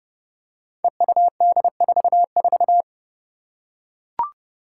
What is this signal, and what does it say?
Morse code, heard as a keyed tone, about 32 wpm, 700 Hz. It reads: EUD44
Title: Five Challenging Letter-Number Combinations 30wpm